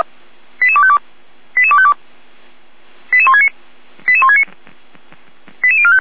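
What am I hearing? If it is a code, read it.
Five-tone selective call (ZVEI), heard as quick runs of five tones, tones of about 70 ms. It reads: 88252 88252 88169 88169 88045
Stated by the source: ZVEI five-tone call received on SDR.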